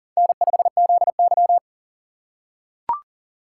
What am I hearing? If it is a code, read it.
Morse code, heard as a keyed tone, about 39 wpm, 700 Hz. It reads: N5ZY
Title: USA Call Signs - 4 Characters - 3x Character Spacing - 40wpm